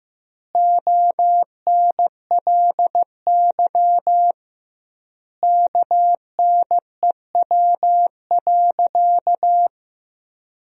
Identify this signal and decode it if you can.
Morse code, heard as a keyed tone, about 15 wpm, 700 Hz. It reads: ONLY KNEW.
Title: 2x Word-Spacing - Sentences from Top 400 Words - 15wpm